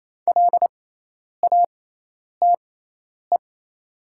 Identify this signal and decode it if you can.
Morse code, heard as a keyed tone, about 28 wpm, spaced as for 10 wpm, 700 Hz. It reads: LATE
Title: 6x Character Spacing - Top 500 Words - 28wpm